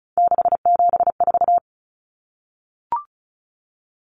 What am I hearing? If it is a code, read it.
Morse code, heard as a keyed tone, about 34 wpm, 700 Hz. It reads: 674
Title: Common Cut Numbers - 3-Digit 35wpm